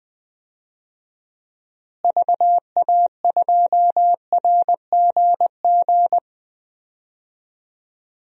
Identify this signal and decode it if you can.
Morse code, heard as a keyed tone, about 20 wpm, 700 Hz. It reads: VA2RGG
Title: DXCC-100 Call Signs - 6 Characters - Repeated 2x - 20wpm